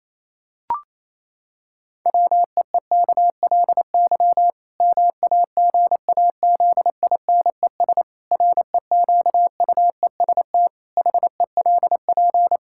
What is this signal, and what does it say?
Morse code, heard as a keyed tone, about 28 wpm, 700 Hz. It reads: WEEKLY MAGAZINES REQUEST HELP
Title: Pangram Challenge v2 - 28wpm